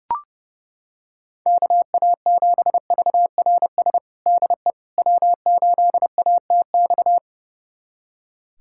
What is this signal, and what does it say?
Morse code, heard as a keyed tone, about 30 wpm, 700 Hz. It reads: KA7VRS DE W8ATX
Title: USA Callsign DE Callsign 30wpm